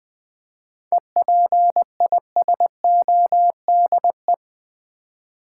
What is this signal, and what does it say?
Morse code, heard as a keyed tone, about 20 wpm, 700 Hz. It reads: EPISODE
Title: 7 Letter Words - First 3 Letters Given - 20wpm